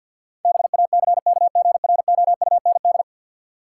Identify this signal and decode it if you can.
Morse code, heard as a keyed tone, about 53 wpm, 700 Hz. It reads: BACKGROUND
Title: Sentences from Top 1300 Words - New Word Drill - 50wpm